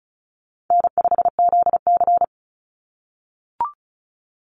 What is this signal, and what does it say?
Morse code, heard as a keyed tone, about 34 wpm, 700 Hz. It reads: N5ZC
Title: NA SOTA Chaser Callsigns 35wpm